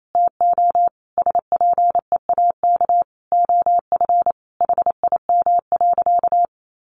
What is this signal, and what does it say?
Morse code, heard as a keyed tone, about 28 wpm, 700 Hz. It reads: TO SPEAK OF HIM.
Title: Sentences from Top 1100 Words 28wpm